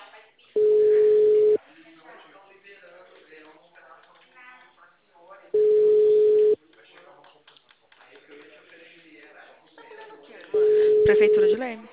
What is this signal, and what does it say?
Ringback tone (Europe, 425 Hz), with speech.
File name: out- -2011- - - .7549